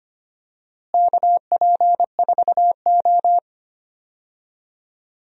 Morse code, heard as a keyed tone, about 25 wpm, 700 Hz. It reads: KP4O
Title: DXCC-100 Call Signs - 4 Characters - Repeated 3x - 25wpm